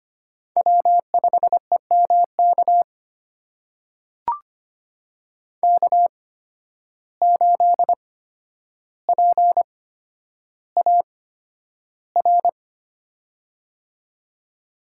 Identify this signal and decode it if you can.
Morse code, heard as a keyed tone, about 25 wpm, spaced as for 11 wpm, 700 Hz. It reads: W5EMK K8PAR
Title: USA Call Signs - 5 Characters - 8x Character Spacing - 25wpm